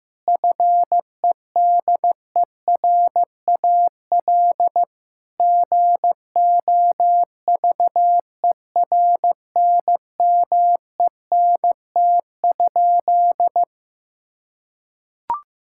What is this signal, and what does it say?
Morse code, heard as a keyed tone, about 15 wpm, 700 Hz. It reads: FEDERAL GOVERNMENT?
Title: Sentences from Top 1000 Words - Review Part 5 - 15wpm